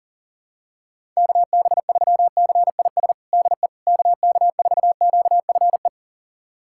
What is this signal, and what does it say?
Morse code, heard as a keyed tone, about 40 wpm, 700 Hz. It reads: KB3CIS DE KK4QFE
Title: USA Callsign DE Callsign - 6 Characters - 40wpm